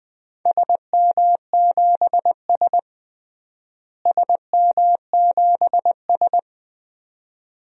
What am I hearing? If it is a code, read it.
Morse code, heard as a keyed tone, about 21 wpm, 700 Hz. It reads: SM7S SM7S
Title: DXCC-100 Call Signs - 4 Characters - Repeated 2x - 20wpm